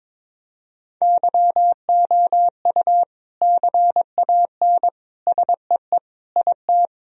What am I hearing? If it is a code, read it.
Morse code, heard as a keyed tone, about 22 wpm, 700 Hz. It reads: YOU CAN SEE IT
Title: Sentences from Top 1000 Words - Review Part 1 - 22wpm